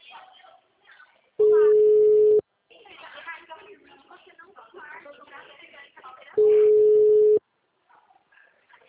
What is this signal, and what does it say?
Ringback tone (Europe, 425 Hz), with speech.